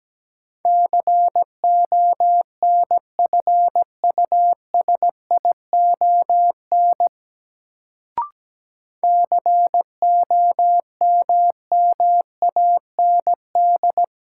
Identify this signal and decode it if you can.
Morse code, heard as a keyed tone, about 17 wpm, 700 Hz. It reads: CONFUSION COMMAND
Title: Sentences from Top 1600 Words - New Word Drill - 17wpm